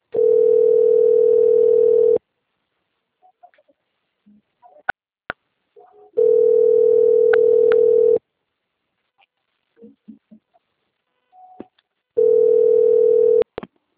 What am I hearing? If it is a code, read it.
Ringback tone (North America), 440 + 480 Hz.